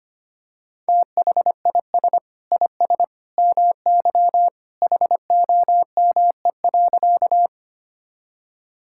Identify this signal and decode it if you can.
Morse code, heard as a keyed tone, about 25 wpm, 700 Hz. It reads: THIS IS MY HOME.
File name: practice / 25wpm / Sentences with 2-Word Phrases - 25wpm